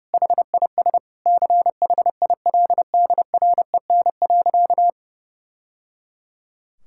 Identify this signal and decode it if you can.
Morse code, heard as a keyed tone, about 32 wpm, 700 Hz. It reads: HIS CHILDREN.
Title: Sentences from Top 1000 Words - Review Part 5 - 30wpm